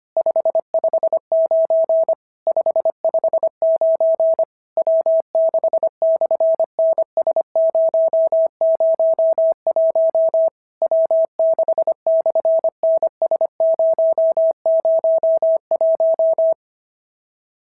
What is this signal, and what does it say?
Morse code, heard as a keyed tone, about 25 wpm, 690 Hz. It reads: 559 559 W6/NS001 W6/NS001
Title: SOTA QSO Practice 25wpm